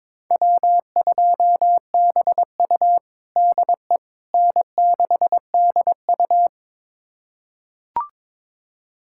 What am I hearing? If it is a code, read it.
Morse code, heard as a keyed tone, about 22 wpm, 700 Hz. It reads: W2BU DE N6DU
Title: USA Callsign DE Callsign - 4 Characters - 22wpm